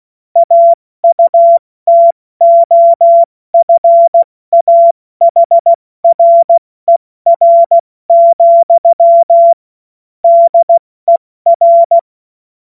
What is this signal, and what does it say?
Morse code, heard as a keyed tone, about 16 wpm, 680 Hz. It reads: AUTOFAHRER, DER
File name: Morsekurs / Witze / Witz 28. Dezember 2021, 70 F80